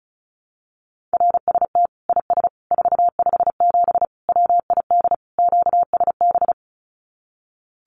Morse code, heard as a keyed tone, about 34 wpm, 700 Hz. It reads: RST IS 457 WID QSB
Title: Common QSO Phrases 35wpm